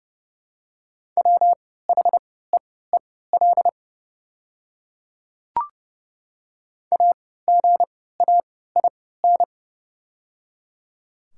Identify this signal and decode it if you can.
Morse code, heard as a keyed tone, about 31 wpm, spaced as for 17 wpm, 700 Hz. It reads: WHEEL AGAIN
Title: 3x Character Spacing - Top 500 Words - 30wpm